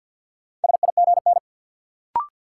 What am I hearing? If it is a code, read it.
Morse code, heard as a keyed tone, about 46 wpm, 700 Hz. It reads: SIGN
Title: Sentences from Top 1000 Words - New Word Drill - 50wpm